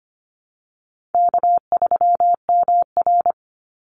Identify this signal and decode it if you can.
Morse code, heard as a keyed tone, about 25 wpm, 700 Hz. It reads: K3MR
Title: Call Signs - 25wpm